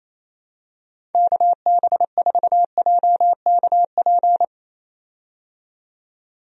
Morse code, heard as a keyed tone, about 28 wpm, 700 Hz. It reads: KB4JKP
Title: USA Call Signs - 6 Characters - 28wpm